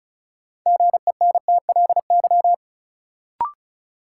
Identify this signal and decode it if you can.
Morse code, heard as a keyed tone, about 36 wpm, 700 Hz. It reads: GENTLY
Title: Sentences from Top 1500 Words - New Word Drill Alt - 35wpm